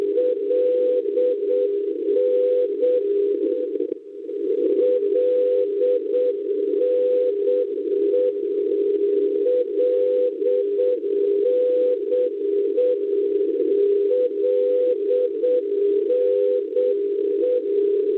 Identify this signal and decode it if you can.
Morse code, heard as a keyed tone, about 7 wpm, 500 Hz, with noise. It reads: LN LNE LNE LNE
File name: LNE NDB Linate Italia 386 kHz 2206 utc Schwarzsee ALA 1530 LNP